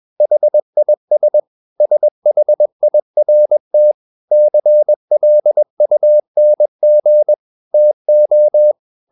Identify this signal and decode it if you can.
Morse code, heard as a keyed tone, about 21 wpm, 600 Hz. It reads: HIS SHIRT CLUNG TO
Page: CW med Gnister